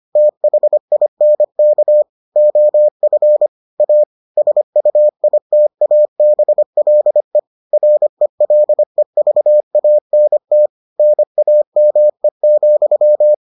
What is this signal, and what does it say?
Morse code, heard as a keyed tone, about 25 wpm, 600 Hz. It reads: THINK OF A SUITABLE RELEVANT NAME,